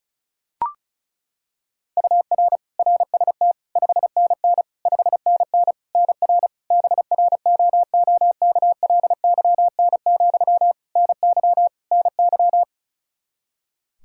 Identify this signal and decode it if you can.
Morse code, heard as a keyed tone, about 36 wpm, 700 Hz. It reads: UR RST 5NN 5NN NR BROOKLYN, NY NY
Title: K1USN SST QSO Practice 35wpm